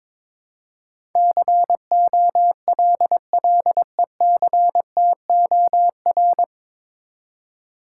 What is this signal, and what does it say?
Morse code, heard as a keyed tone, about 22 wpm, 700 Hz. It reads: COLLECTOR
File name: 9 Letter Words - First 3 Letters Given - 22wpm